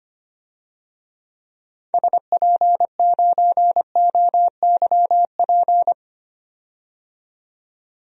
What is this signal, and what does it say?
Morse code, heard as a keyed tone, about 25 wpm, 700 Hz. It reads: SP9OYP
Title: DXCC-100 Call Signs - 6 Characters - 25wpm